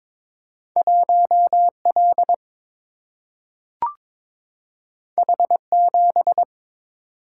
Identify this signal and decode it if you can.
Morse code, heard as a keyed tone, about 21 wpm, 700 Hz. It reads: 1L H7
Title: Single-Letter Single-Number 22wpm